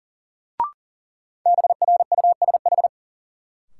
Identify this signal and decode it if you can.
Morse code, heard as a keyed tone, about 40 wpm, 700 Hz. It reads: BRUSH
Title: Sentences from Top 1100 Words - New Word Drill - 40wpm